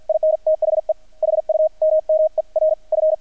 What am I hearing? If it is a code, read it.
Morse code, heard as a keyed tone, about 50 wpm, spaced as for 43 wpm, 630 Hz. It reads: IN THE HUMMER F
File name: archived / CRSnet